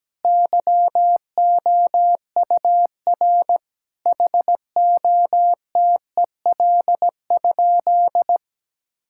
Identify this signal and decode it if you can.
Morse code, heard as a keyed tone, about 17 wpm, 700 Hz. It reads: YOUR HOTEL?
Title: Sentences from Top 1600 Words 17wpm